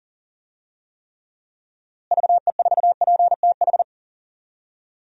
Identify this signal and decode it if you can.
Morse code, heard as a keyed tone, about 38 wpm, 700 Hz. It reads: VE4PTH
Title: DXCC-100 Call Signs - 6 Characters - Repeated 2x - 40wpm